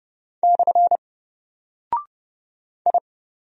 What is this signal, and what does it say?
Morse code, heard as a keyed tone, about 30 wpm, 700 Hz. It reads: / I
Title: Single Character (Letter:Number:Punctuation:Prosign) Alt 30wpm